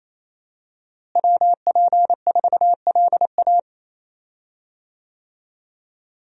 Morse code, heard as a keyed tone, about 28 wpm, 700 Hz. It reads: WP4LA
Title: USA Call Signs - 5 Characters - Repeated 3x - 28wpm